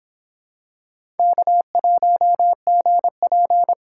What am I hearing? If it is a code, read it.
Morse code, heard as a keyed tone, about 23 wpm, 700 Hz. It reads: K1GP